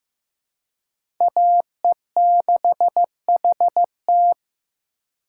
Morse code, heard as a keyed tone, about 15 wpm, 700 Hz. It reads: AE6HT
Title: USA Call Signs - 5 Characters - 4x Character Spacing - 15wpm